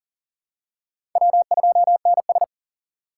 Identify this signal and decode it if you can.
Morse code, heard as a keyed tone, about 40 wpm, 700 Hz. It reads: W2NS
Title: USA Call Signs - 4 Characters - 6x Character Spacing - 40wpm